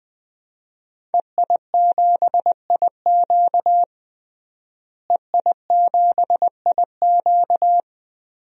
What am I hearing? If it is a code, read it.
Morse code, heard as a keyed tone, about 20 wpm, 700 Hz. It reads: EI7IQ EI7IQ
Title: DXCC-100 Call Signs - 5 Characters - Repeated 3x - 20wpm